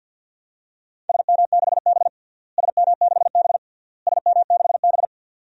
Morse code, heard as a keyed tone, about 50 wpm, 700 Hz. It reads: SM6B SM6B SM6B
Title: DXCC-100 Call Signs - 4 Characters - Repeated 3x - 50wpm